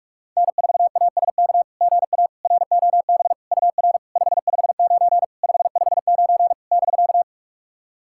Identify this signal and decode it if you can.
Morse code, heard as a keyed tone, about 46 wpm, 700 Hz. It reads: N4ASK GA ROB UR 559 559 <BK>
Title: SOTA QSO Practice 45wpm